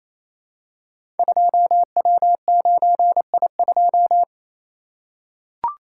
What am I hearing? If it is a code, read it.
Morse code, heard as a keyed tone, about 28 wpm, 700 Hz. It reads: 2W9I2
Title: Code Groups - 5 Letter:Numbers - 28wpm